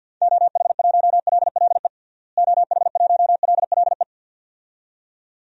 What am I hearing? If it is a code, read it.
Morse code, heard as a keyed tone, about 51 wpm, 700 Hz. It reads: OH1LLE OH1LLE
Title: DXCC-100 Call Signs - 6 Characters - Repeated 2x - 50wpm